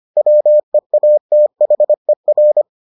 Morse code, heard as a keyed tone, about 25 wpm, 600 Hz. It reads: WEATHER